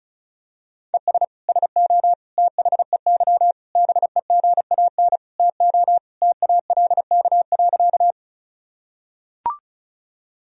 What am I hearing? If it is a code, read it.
Morse code, heard as a keyed tone, about 36 wpm, 700 Hz. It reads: ES SO THEY BEGAN TO TALK.
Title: Sentences from Top 400 Words and Abbreviations 35wpm